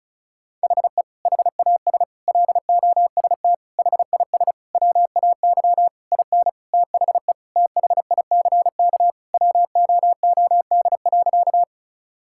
Morse code, heard as a keyed tone, about 36 wpm, 700 Hz. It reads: HE HAS LOST HIS WAY IN THE THICK WOOD.
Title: Sentences from Top 1600 Words 35wpm